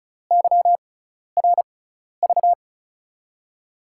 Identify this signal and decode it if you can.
Morse code, heard as a keyed tone, about 36 wpm, spaced as for 12 wpm, 700 Hz. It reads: YRV
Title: USA Call Signs - 6 Characters - 6x Character Spacing - 35wpm